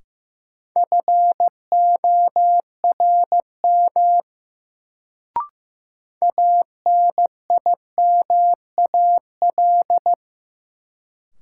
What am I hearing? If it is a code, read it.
Morse code, heard as a keyed tone, about 15 wpm, 700 Hz. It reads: FORM ANIMAL
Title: Sentences from Top 1000 Words - Review Part 1 - New Word Drill - 15wpm